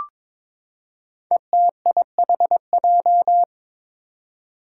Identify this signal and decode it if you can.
Morse code, heard as a keyed tone, about 22 wpm, 700 Hz. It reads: ETIHJ
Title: Five Challenging Letter-Number Combinations 22wpm